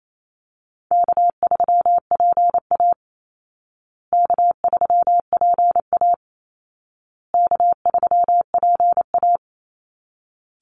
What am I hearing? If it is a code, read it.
Morse code, heard as a keyed tone, about 28 wpm, 700 Hz. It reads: K3PA K3PA K3PA
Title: Call Signs Repeated 2x - 28wpm